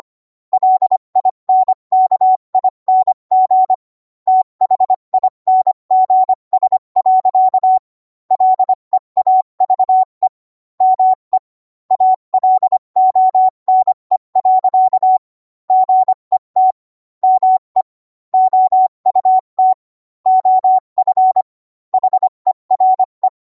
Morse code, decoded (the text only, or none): LINKING THINGS. LEAVE ME ALONE. GET ME OUT OF HERE